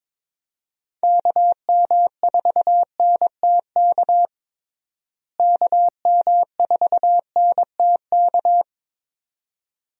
Morse code, heard as a keyed tone, about 22 wpm, 700 Hz. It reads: KM4NTK KM4NTK
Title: USA Call Signs - 6 Characters - Repeated 2x - 22wpm